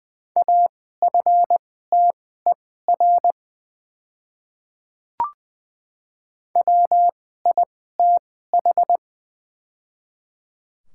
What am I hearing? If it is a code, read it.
Morse code, heard as a keyed tone, about 20 wpm, spaced as for 15 wpm, 690 Hz. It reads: AFTER WITH
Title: 2x Character Spacing - Top 200 Words - 20wpm